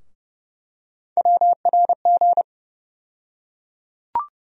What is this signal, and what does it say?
Morse code, heard as a keyed tone, about 30 wpm, 700 Hz. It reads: WRG
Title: USA Call Sign Postfixes - 3 Letters 30wpm